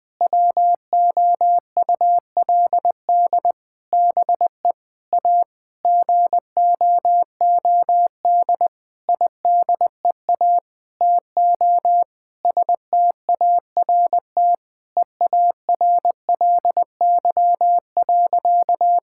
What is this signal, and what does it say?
Morse code, heard as a keyed tone, about 20 wpm, 700 Hz. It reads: WOULD BE A GOOD IDEA TO START EARLY.
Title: Sentences with 8-Word Phrases - Spoken Beginning - 20wpm